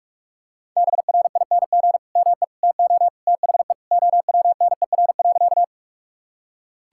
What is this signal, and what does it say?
Morse code, heard as a keyed tone, about 46 wpm, 700 Hz. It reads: BRING ME TO THE OWNER.
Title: Sentences from Top 1600 Words 45wpm